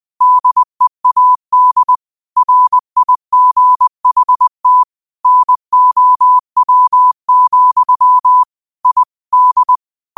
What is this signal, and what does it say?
Morse code, heard as a keyed tone, about 20 wpm, 1000 Hz. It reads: DEAD RIGHT NOW, I D
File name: qotdfiles / 20wpm